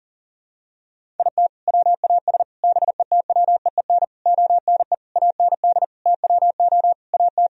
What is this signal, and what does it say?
Morse code, heard as a keyed tone, about 37 wpm, 700 Hz. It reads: IT WAS BETWEEN ONE AND TWO AT
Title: Sentences from Top 1000 Words - Review Part 1 - 40wpm